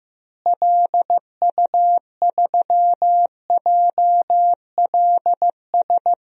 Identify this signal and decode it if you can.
Morse code, heard as a keyed tone, about 15 wpm, 700 Hz. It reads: LU3JLS
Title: DXCC-100 Call Signs - 6 Characters - Repeated 3x - 15wpm